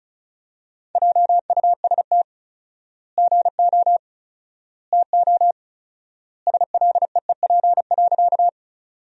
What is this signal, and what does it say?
Morse code, heard as a keyed tone, about 34 wpm, 700 Hz. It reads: JUST GO TO SLEEP.
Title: 4x Word-Spacing - Sentences from Top 500 Words - 35wpm